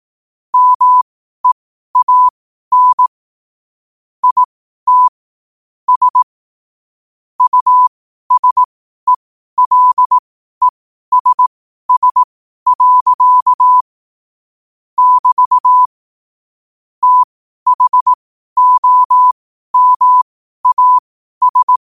Morse code, heard as a keyed tone, about 18 wpm, spaced as for 12 wpm, 1000 Hz. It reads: MEAN IT S USELESS. <BT> THOMAS
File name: qotdfiles / 12wpm